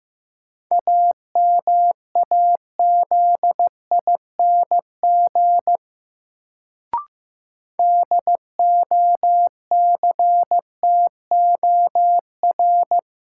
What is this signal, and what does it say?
Morse code, heard as a keyed tone, about 15 wpm, 700 Hz. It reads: AMAZING DOCTOR
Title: Sentences from Top 1500 Words - New Word Drill Alt - 15wpm